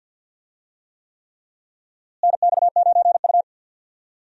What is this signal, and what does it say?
Morse code, heard as a keyed tone, about 53 wpm, 700 Hz. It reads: NX9U
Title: USA Call Signs - 4 Characters - Repeated 3x - 50wpm